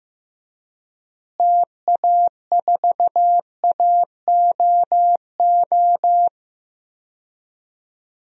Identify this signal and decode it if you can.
Morse code, heard as a keyed tone, about 15 wpm, 700 Hz. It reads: TA4AOO
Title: DXCC-100 Call Signs - 6 Characters - Repeated 3x - 15wpm